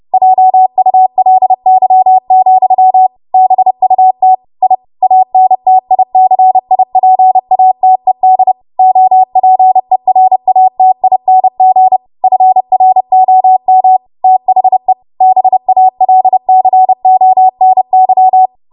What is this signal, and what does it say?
Morse code, heard as a keyed tone, about 30 wpm, 750 Hz. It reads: JULY, BUT I ANTICIPATED OPERATING FROM THE BALCONY
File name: Morse / 30WPM